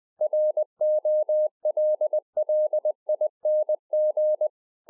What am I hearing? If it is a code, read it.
Morse code, heard as a keyed tone, about 20 wpm, 610 Hz. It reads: ROLLING